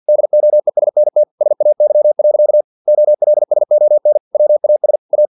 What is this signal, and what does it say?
Morse code, heard as a keyed tone, about 50 wpm, 600 Hz. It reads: DOESNT SAY. OLSON WAS A
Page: Never 49wpm